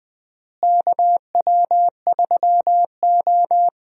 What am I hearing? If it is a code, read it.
Morse code, heard as a keyed tone, about 20 wpm, 700 Hz. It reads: KW3O
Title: USA Call Signs - 4 Characters - Repeated 3x - 20wpm